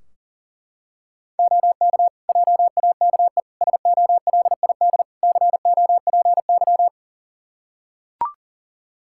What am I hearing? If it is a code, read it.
Morse code, heard as a keyed tone, about 40 wpm, 700 Hz. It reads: OK JAKE SOLID COPY
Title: Practice QSO - Exchange 3 - More Info - 40wpm